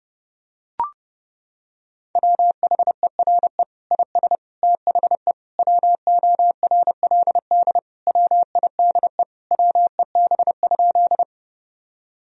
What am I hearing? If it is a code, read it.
Morse code, heard as a keyed tone, about 30 wpm, 700 Hz. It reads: WHERE IS THE WORLD WIDE WEB?
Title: Sentences from Top 1000 Words - Review Part 4 - 30wpm